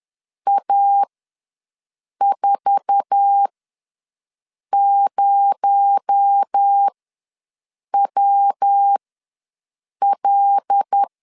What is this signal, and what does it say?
Morse code, heard as a keyed tone, about 10 wpm, spaced as for 6 wpm, 800 Hz. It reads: A40WL
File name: LEZIONI / 8 YJ45 / 8 YJ4 ws 10wpm 4020